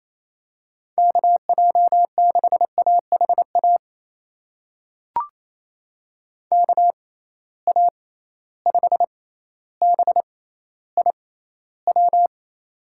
Morse code, heard as a keyed tone, about 28 wpm, spaced as for 14 wpm, 700 Hz. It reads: KJ6AHA KA5BIW
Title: USA Call Signs - 6 Characters - 6x Character Spacing - 28wpm